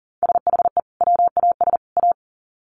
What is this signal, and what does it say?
Morse code, heard as a keyed tone, about 40 wpm, 700 Hz. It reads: SHE WAS A
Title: Sets of 3 Words 40wpm